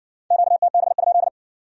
Morse code, heard as a keyed tone, about 54 wpm, 700 Hz. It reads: QTB?